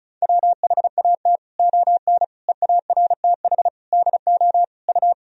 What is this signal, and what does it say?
Morse code, heard as a keyed tone, about 36 wpm, 700 Hz. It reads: WHAT ON EARTH DO U
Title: Sentences from Top 200 Words and Abbreviations 35wpm